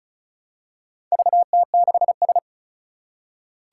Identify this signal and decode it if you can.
Morse code, heard as a keyed tone, about 36 wpm, 700 Hz. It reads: VT6S
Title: Four Challenging Letter-Number Combinations 35wpm